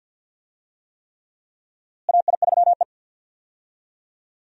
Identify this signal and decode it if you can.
Morse code, heard as a keyed tone, about 51 wpm, 700 Hz. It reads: AI3E
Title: USA Call Signs - 4 Characters - Repeated 3x - 50wpm